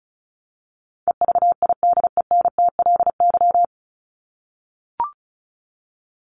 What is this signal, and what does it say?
Morse code, heard as a keyed tone, about 34 wpm, 700 Hz. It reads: EVIDENTLY
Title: LY Suffix - Top 100 Words with Base Words 35wpm